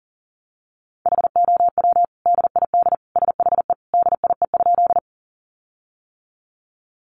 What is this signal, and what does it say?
Morse code, heard as a keyed tone, about 40 wpm, 700 Hz. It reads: HOW DID SHE DIE?
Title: Sentences from Top 1300 Words 40wpm